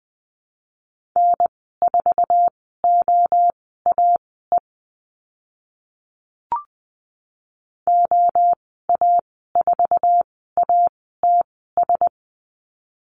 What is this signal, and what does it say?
Morse code, heard as a keyed tone, about 20 wpm, spaced as for 14 wpm, 700 Hz. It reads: N4OAE OA4ATS
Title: 076 Lesson 13g - Letter L - Call Signs 20wpm